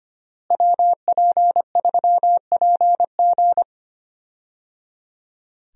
Morse code, heard as a keyed tone, about 25 wpm, 700 Hz. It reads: WP3PG
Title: USA Call Signs - 5 Characters - Repeated 3x - 25wpm